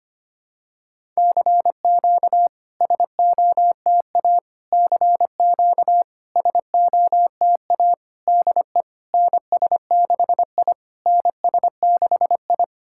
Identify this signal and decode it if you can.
Morse code, heard as a keyed tone, about 25 wpm, 700 Hz. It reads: CQ SOTA CQ SOTA DE NS6I NS6I
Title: SOTA QSO Practice 25wpm